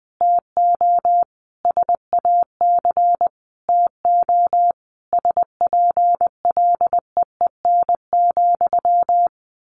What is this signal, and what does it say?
Morse code, heard as a keyed tone, about 20 wpm, 700 Hz. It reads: TO SAC TO SPLEEN,
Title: A Sound Of Thunder 20wpm